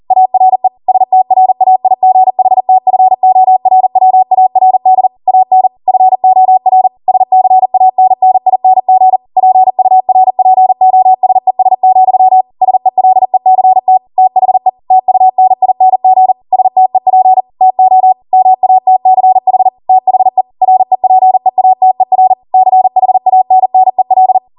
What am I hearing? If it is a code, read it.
Morse code, heard as a keyed tone, about 40 wpm, 750 Hz. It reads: ARE STRAIGHTFORWARD AN FOR SCANNING PURPOSES, SELECT THE TUNING STEP TO MATCH THE REPEATER CHANNEL
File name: Morse / 40WPM